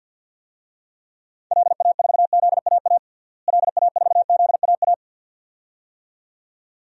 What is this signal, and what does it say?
Morse code, heard as a keyed tone, about 47 wpm, 700 Hz. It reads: LA4ZAA LA4ZAA
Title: DXCC-100 Call Signs - 6 Characters - Repeated 2x - 50wpm